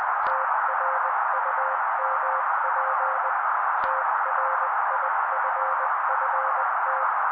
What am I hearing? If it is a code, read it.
Morse code, heard as a keyed tone, about 20 wpm, 530 Hz. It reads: TRUMP TRIFFT
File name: daily / 18wpm noise